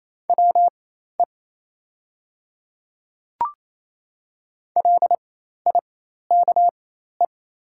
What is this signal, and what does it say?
Morse code, heard as a keyed tone, about 27 wpm, spaced as for 13 wpm, 690 Hz. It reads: WE LIKE
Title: 4x Character Spacing - Top 200 Words - 28wpm